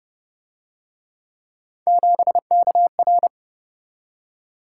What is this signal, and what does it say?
Morse code, heard as a keyed tone, about 24 wpm, 700 Hz. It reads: 7KR
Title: Speed-Racing - Three Letter-Number Combinations - 20wpm